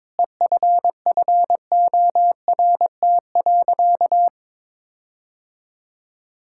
Morse code, heard as a keyed tone, about 21 wpm, 700 Hz. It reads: EFFORT.
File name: Sentences with 7-Word Phrases - 22wpm